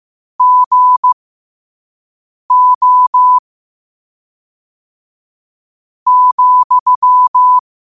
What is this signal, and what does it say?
Morse code, heard as a keyed tone, about 15 wpm, spaced as for 5 wpm, 1000 Hz. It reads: GO ,
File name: qotdfiles / 5wpm